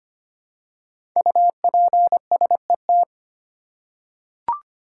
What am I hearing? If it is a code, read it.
Morse code, heard as a keyed tone, about 25 wpm, 700 Hz. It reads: UPSET
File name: 5 Letter Words - First 3 Letters Given - 25wpm